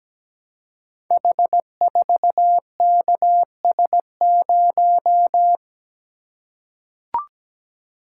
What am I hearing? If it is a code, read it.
Morse code, heard as a keyed tone, about 17 wpm, 700 Hz. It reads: H4KS0
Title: Code Groups - 5 Letter:Numbers - 17wpm